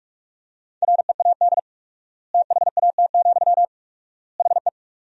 Morse code, heard as a keyed tone, about 42 wpm, 700 Hz. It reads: READ THAT, HE
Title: 4x Word-Spacing - Sentences from Top 200 Words - 45wpm